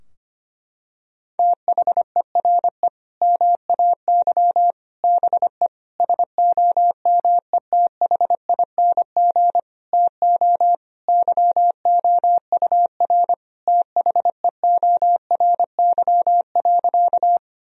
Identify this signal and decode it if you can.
Morse code, heard as a keyed tone, about 25 wpm, 700 Hz. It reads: THERE MAY BE SOMETHING TO YOUR THEORY.
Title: Sentences from Top 1000 Words - Review Part 3 - 25wpm